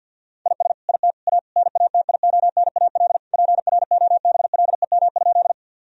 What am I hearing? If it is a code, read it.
Morse code, heard as a keyed tone, about 50 wpm, 700 Hz. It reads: IS IT A NATIONAL PROBLEM?
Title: Sentences from Top 1000 Words - Review Part 4 - 50wpm